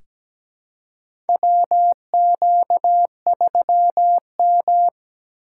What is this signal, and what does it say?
Morse code, heard as a keyed tone, about 17 wpm, 700 Hz. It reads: WQ3M
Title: USA Call Signs - 4 Characters - 6x Character Spacing - 17wpm